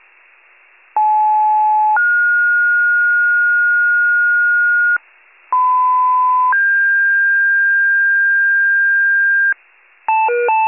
Two-tone page, heard as a one-second tone, then a three-second tone, about 860 then 1430 Hz, about 1010 then 1600 Hz, with noise. It is heard from 1.0 to 9.5 s.